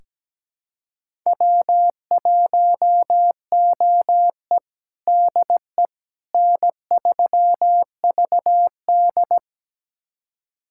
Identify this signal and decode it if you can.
Morse code, heard as a keyed tone, about 17 wpm, 700 Hz. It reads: W1OE DE N3VD
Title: USA Callsign DE Callsign - 4 Characters - 17wpm